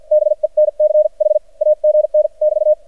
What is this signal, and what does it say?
Morse code, heard as a keyed tone, about 53 wpm, spaced as for 43 wpm, 610 Hz. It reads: BENCH AGN <BT>
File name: archived / CRSnet